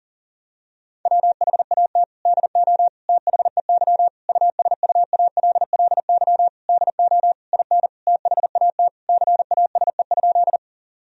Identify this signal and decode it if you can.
Morse code, heard as a keyed tone, about 40 wpm, 700 Hz. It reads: WHAT DO THEY USUALLY DO IN THAT CASE?
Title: Sentences from Top 1000 Words - Review Part 3 - 40wpm